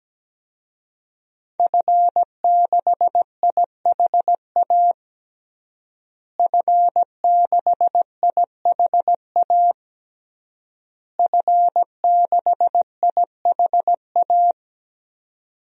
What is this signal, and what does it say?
Morse code, heard as a keyed tone, about 17 wpm, 700 Hz. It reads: F6IHA F6IHA F6IHA
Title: DXCC-100 Call Signs - 5 Characters - Repeated 3x - 17wpm